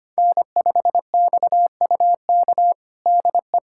Morse code, heard as a keyed tone, about 25 wpm, 700 Hz. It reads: N5XUK DE
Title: USA Callsign DE Callsign - 5 Characters - 25wpm